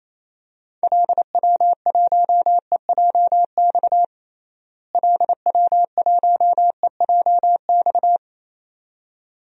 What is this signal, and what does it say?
Morse code, heard as a keyed tone, about 28 wpm, 700 Hz. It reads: LW1EJX LW1EJX
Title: DXCC-100 Call Signs - 6 Characters - Repeated 2x - 28wpm